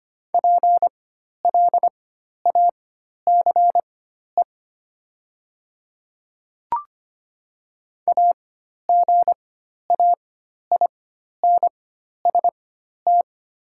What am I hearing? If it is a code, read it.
Morse code, heard as a keyed tone, about 25 wpm, spaced as for 12 wpm, 700 Hz. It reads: PLACE AGAINST
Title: 4x Character Spacing - Top 500 Words - 25wpm